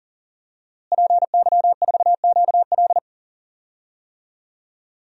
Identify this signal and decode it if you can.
Morse code, heard as a keyed tone, about 40 wpm, 700 Hz. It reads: PY4QL
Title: DXCC-100 Call Signs - 5 Characters - Repeated 2x - 40wpm